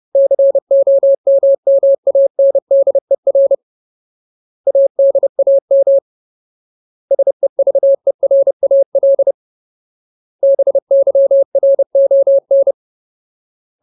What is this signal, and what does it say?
Morse code, heard as a keyed tone, about 30 wpm, 550 Hz. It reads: COMMANDER ADAM SEVERAL BYRON
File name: Ultimate Word List / 30WPM Ultimate